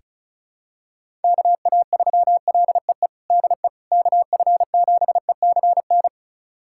Morse code, heard as a keyed tone, about 34 wpm, 700 Hz. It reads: KA3LEE DE KF7ECN
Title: USA Callsign DE Callsign - 6 Characters - 35wpm